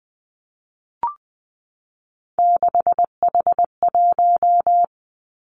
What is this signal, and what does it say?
Morse code, heard as a keyed tone, about 20 wpm, 700 Hz. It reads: 6H1
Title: Three Challenging Letter-Number Combinations 20wpm